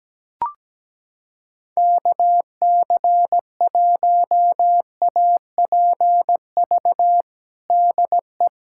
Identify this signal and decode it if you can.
Morse code, heard as a keyed tone, about 17 wpm, 700 Hz. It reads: KC1APV DE
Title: USA Callsign DE Callsign 17wpm